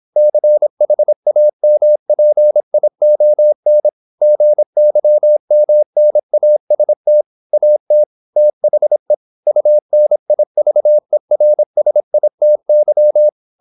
Morse code, heard as a keyed tone, about 26 wpm, 600 Hz. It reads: CHAMPION GYMNAST AT THE UNIVERSITY
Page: Never 26wpm | CW med Gnister